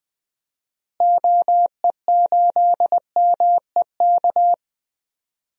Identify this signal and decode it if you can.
Morse code, heard as a keyed tone, about 20 wpm, 700 Hz. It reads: OE8MEK
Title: DXCC-100 Call Signs - 6 Characters - Repeated 2x - 20wpm